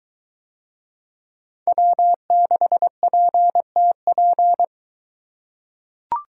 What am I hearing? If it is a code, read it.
Morse code, heard as a keyed tone, about 23 wpm, 700 Hz. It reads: W6PTP